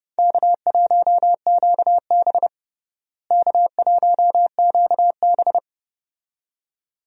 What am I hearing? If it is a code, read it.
Morse code, heard as a keyed tone, about 30 wpm, 700 Hz. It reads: K1QB K1QB
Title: USA Call Signs - 4 Characters - Repeated 2x - 30wpm